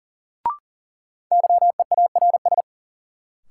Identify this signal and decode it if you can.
Morse code, heard as a keyed tone, about 40 wpm, 700 Hz. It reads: YEARS
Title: Sets of 2 Words - New Word Drill - 40wpm